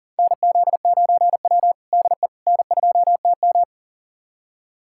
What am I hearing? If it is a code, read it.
Morse code, heard as a keyed tone, about 40 wpm, 700 Hz. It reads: NZ9W DE N2TM
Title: USA Callsign DE Callsign - 4 Characters - 40wpm